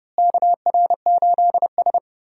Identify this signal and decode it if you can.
Morse code, heard as a keyed tone, about 30 wpm, 700 Hz. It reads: KR8S
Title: USA Call Signs - 4 Characters - Repeated 3x - 30wpm